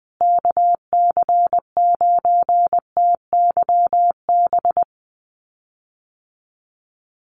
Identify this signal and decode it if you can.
Morse code, heard as a keyed tone, about 20 wpm, 700 Hz. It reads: KC9TYB
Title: USA Call Signs Repeated 3x 20wpm